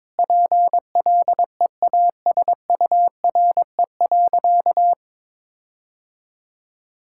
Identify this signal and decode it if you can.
Morse code, heard as a keyed tone, about 22 wpm, 700 Hz. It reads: PLEASURE.
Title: Sentences from Top 1600 Words 22wpm